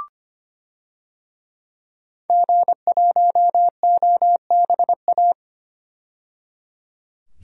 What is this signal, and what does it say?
Morse code, heard as a keyed tone, about 25 wpm, 700 Hz. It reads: G1OBA
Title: DXCC-100 Call Signs - 5 Characters - 25wpm